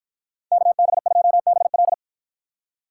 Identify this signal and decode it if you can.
Morse code, heard as a keyed tone, about 53 wpm, 700 Hz. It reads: KB2BL